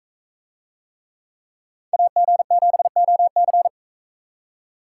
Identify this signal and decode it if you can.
Morse code, heard as a keyed tone, about 44 wpm, 700 Hz. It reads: AG7OC